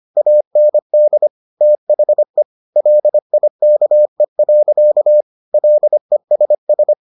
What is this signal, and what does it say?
Morse code, heard as a keyed tone, about 25 wpm, 600 Hz. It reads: AND THE LIKE. LESS